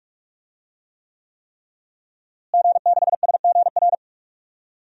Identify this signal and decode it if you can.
Morse code, heard as a keyed tone, about 44 wpm, 700 Hz. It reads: G6SGR
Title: DXCC-100 Call Signs - 5 Characters - Repeated 3x - 45wpm